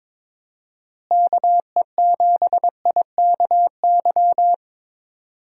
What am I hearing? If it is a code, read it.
Morse code, heard as a keyed tone, about 22 wpm, 700 Hz. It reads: KE7IKY
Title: USA Call Signs - 6 Characters - Repeated 2x - 22wpm